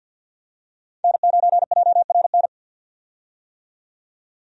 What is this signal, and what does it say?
Morse code, heard as a keyed tone, about 50 wpm, 700 Hz. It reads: N9JRN
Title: USA Call Signs - 5 Characters - 50wpm